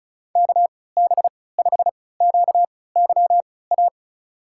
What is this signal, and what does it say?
Morse code, heard as a keyed tone, about 36 wpm, spaced as for 20 wpm, 700 Hz. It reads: KB5QYA
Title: USA Call Signs - 6 Characters - 3x Character Spacing - 35wpm